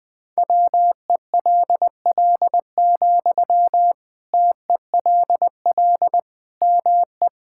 Morse code, heard as a keyed tone, about 20 wpm, 700 Hz. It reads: WELL, TELL ME
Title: Sentences from Top 1000 Words - Review Part 6 - 20wpm